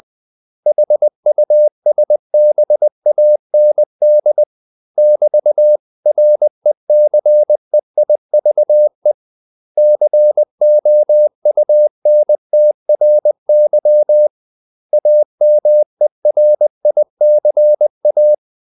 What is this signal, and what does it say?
Morse code, heard as a keyed tone, about 20 wpm, 600 Hz. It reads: HUSBAND <BT>RECEIVE COUNTRY AMERICA
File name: CW / Practice / 500MostCommonEnglishWords20